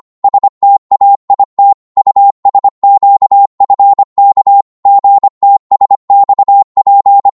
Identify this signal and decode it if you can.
Morse code, heard as a keyed tone, about 25 wpm, 800 Hz. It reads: STAIT USQFK GTSXP
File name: Morse / ltr-25wpm